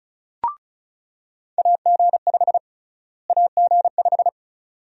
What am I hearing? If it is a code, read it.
Morse code, heard as a keyed tone, about 36 wpm, 700 Hz. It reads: AG5 AG5
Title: USA Call Sign Prefixes - Repeated 2x - 35wpm